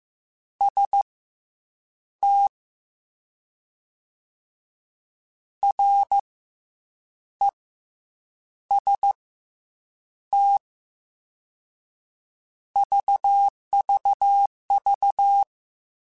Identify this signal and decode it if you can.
Morse code, heard as a keyed tone, about 15 wpm, spaced as for 7 wpm, 800 Hz. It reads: ST REST VVV
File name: Combinations / Combinations-mp3s / Combinations@13